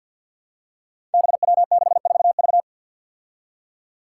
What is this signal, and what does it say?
Morse code, heard as a keyed tone, about 53 wpm, 700 Hz. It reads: BW64V
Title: Five Challenging Letter-Number Combinations 50wpm